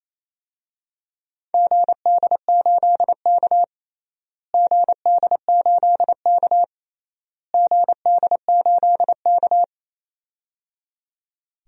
Morse code, heard as a keyed tone, about 28 wpm, 700 Hz. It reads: GD8K GD8K GD8K
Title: DXCC-100 Call Signs - 4 Characters - Repeated 3x - 28wpm